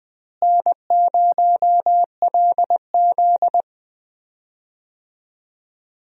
Morse code, heard as a keyed tone, about 20 wpm, 700 Hz. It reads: N0LZ
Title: USA Call Signs - 4 Characters - Repeated 2x - 20wpm